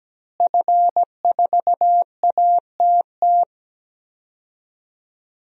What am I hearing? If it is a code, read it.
Morse code, heard as a keyed tone, about 17 wpm, 700 Hz. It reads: F4ATT
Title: DXCC-100 Call Signs - 5 Characters - Repeated 3x - 17wpm